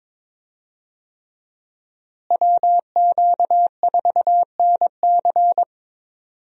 Morse code, heard as a keyed tone, about 22 wpm, 700 Hz. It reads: WQ4NC
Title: USA Call Signs - 5 Characters - Repeated 3x - 22wpm